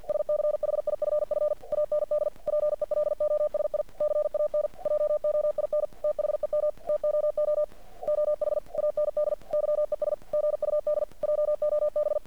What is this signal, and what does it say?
Morse code, heard as a keyed tone, about 50 wpm, 600 Hz. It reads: SCHIFF AND PELOSI CAN JOIN THEM TOO OH AND YES GUD JOB
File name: archived / CRSnet